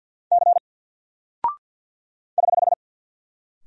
Morse code, heard as a keyed tone, about 50 wpm, 700 Hz. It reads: C <HH>
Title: Single Character (Letter:Number:Punctuation:Prosign) 50wpm